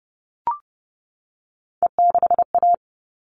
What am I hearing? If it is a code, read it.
Morse code, heard as a keyed tone, about 30 wpm, 700 Hz. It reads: E6A
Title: All Cut Numbers - 3-Digit 30wpm